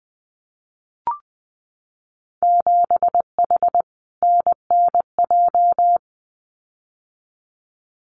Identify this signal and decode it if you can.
Morse code, heard as a keyed tone, about 20 wpm, 700 Hz. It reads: 7H NNJ
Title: Winter Field Day Exchanges Encore 20wpm